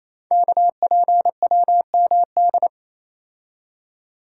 Morse code, heard as a keyed tone, about 29 wpm, 700 Hz. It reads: KPWMD
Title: Code Groups - 5 Letters - 28wpm